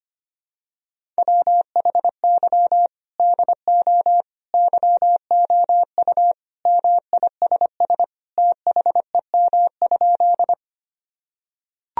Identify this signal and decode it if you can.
Morse code, heard as a keyed tone, about 25 wpm, 700 Hz. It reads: WHY DO YOU MISS THEM?
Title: Sentences from Top 1000 Words - Review Part 3 - 25wpm